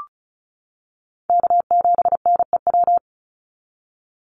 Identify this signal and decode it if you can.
Morse code, heard as a keyed tone, about 34 wpm, 700 Hz. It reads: K7NEW
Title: NA SOTA Chaser Callsigns 35wpm